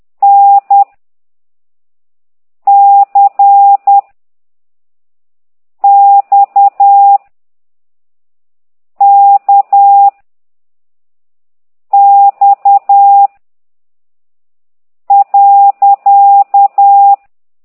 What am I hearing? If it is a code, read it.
Morse code, heard as a keyed tone, about 10 wpm, 800 Hz. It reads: N C X K X .